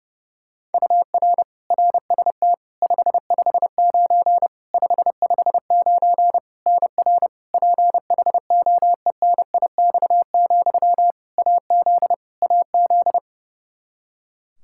Morse code, heard as a keyed tone, about 30 wpm, 700 Hz. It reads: UR RST 559 559 NR PHOENIX, AZ AZ
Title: Practice QSO - Exchange 1 - Beginning - 30wpm